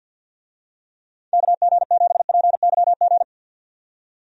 Morse code, heard as a keyed tone, about 51 wpm, 700 Hz. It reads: KG7PYG
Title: USA Call Signs - 6 Characters - 50wpm